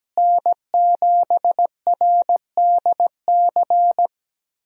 Morse code, heard as a keyed tone, about 17 wpm, 700 Hz. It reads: N7RDC
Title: USA Call Signs - 5 Characters - Repeated 2x - 17wpm